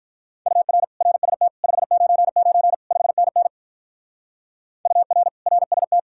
Morse code, heard as a keyed tone, about 44 wpm, 700 Hz. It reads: UR RST 599 5NN UR RST